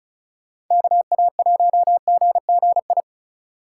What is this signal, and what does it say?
Morse code, heard as a keyed tone, about 39 wpm, 700 Hz. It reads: KA1GGI